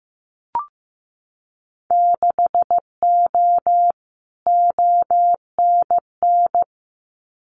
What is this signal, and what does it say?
Morse code, heard as a keyed tone, about 15 wpm, 700 Hz. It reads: 6O ONN
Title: Winter Field Day Exchanges Encore 15wpm